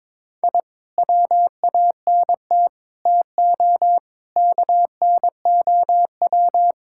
Morse code, heard as a keyed tone, about 22 wpm, 700 Hz. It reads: I WANT TO KNOW
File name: practice / 22wpm / Sentences with 2-Word Phrases - 22wpm